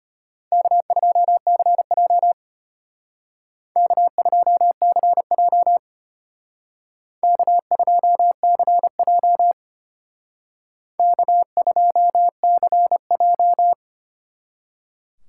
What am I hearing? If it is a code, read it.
Morse code, heard as a keyed tone, about 32 wpm, 700 Hz. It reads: K2CJ K2CJ K2CJ K2CJ